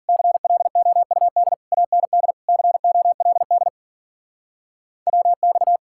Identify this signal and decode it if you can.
Morse code, heard as a keyed tone, about 50 wpm, 700 Hz. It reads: CLOUD AND COLD WX
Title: Speed-Racing Common QSO Phrases - 40wpm